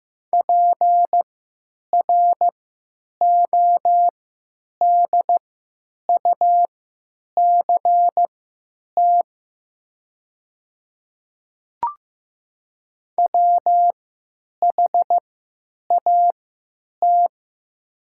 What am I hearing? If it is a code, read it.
Morse code, heard as a keyed tone, about 15 wpm, spaced as for 9 wpm, 700 Hz. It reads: PRODUCT WHAT
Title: 3x Character Spacing - Top 500 Words - 15wpm